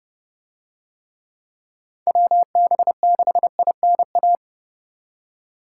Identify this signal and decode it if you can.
Morse code, heard as a keyed tone, about 29 wpm, 700 Hz. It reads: WB6INA